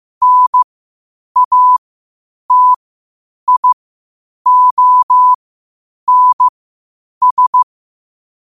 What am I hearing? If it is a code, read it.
Morse code, heard as a keyed tone, about 15 wpm, spaced as for 8 wpm, 1000 Hz. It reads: NATIONS
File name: qotdfiles / 8wpm